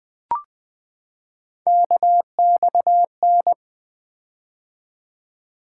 Morse code, heard as a keyed tone, about 20 wpm, 700 Hz. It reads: KXN
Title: Three Letter-Number Combinations - 20wpm